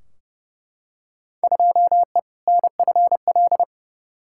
Morse code, heard as a keyed tone, about 31 wpm, 700 Hz. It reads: 2E NFL
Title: Field Day Exchanges Encore - 30wpm